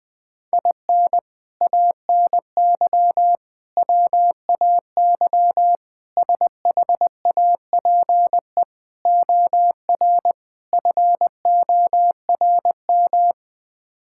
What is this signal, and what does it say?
Morse code, heard as a keyed tone, about 21 wpm, 700 Hz. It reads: IN ANY WAY SHAPE OR FORM
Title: Sets of 6 Words - 20wpm